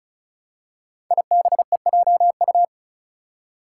Morse code, heard as a keyed tone, about 36 wpm, 700 Hz. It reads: IBEJU
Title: Five Challenging Letter-Number Combinations 35wpm